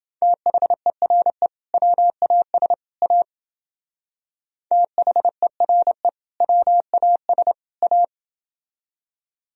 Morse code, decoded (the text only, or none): THERE WAS A THERE WAS A